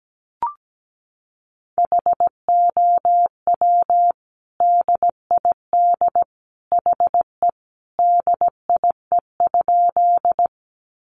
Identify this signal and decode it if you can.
Morse code, heard as a keyed tone, about 17 wpm, 700 Hz. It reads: HOW DID HE DIE?
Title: Sentences from Top 1300 Words 17wpm